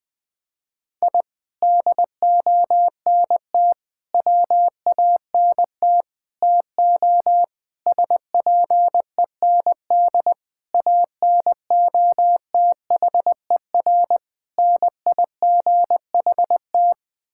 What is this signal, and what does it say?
Morse code, heard as a keyed tone, about 20 wpm, 700 Hz. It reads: I DONT WANT TO SPEND ANOTHER NIGHT
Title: Sentences from Top 1000 Words - Review Part 5 - 20wpm